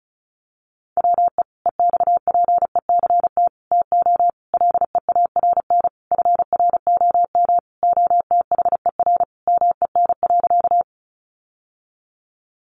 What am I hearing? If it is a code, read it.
Morse code, heard as a keyed tone, about 34 wpm, 700 Hz. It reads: WE EXPECT TO LEARN FROM OTHER MEN.
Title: Sentences from Top 1100 Words 35wpm